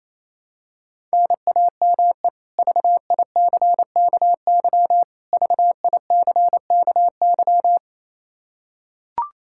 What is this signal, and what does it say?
Morse code, heard as a keyed tone, about 28 wpm, 700 Hz. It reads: NAME VICKY VICKY
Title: K1USN SST QSO Practice 28wpm